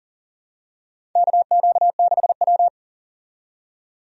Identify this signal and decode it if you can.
Morse code, heard as a keyed tone, about 40 wpm, 700 Hz. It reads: KQ6W
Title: USA Call Signs - 4 Characters - 40wpm